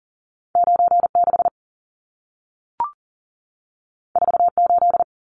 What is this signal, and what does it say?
Morse code, heard as a keyed tone, about 40 wpm, 700 Hz. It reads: 96 48
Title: Numbers - 2-Digit 40wpm